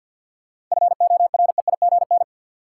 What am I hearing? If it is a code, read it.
Morse code, heard as a keyed tone, about 47 wpm, 700 Hz. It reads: FOREIGN
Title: Sentences from Top 1400 Words - New Word Drill - 50wpm